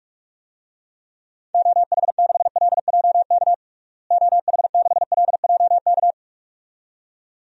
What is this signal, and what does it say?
Morse code, heard as a keyed tone, about 46 wpm, 690 Hz. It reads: OH6LJK OH6LJK
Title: DXCC-100 Call Signs - 6 Characters - Repeated 2x - 45wpm